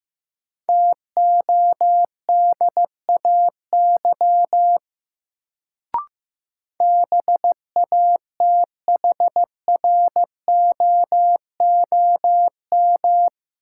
Morse code, heard as a keyed tone, about 15 wpm, 700 Hz. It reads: TODAY BATHROOM
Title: Sets of 5 Words Encore - New Word Drill - 15wpm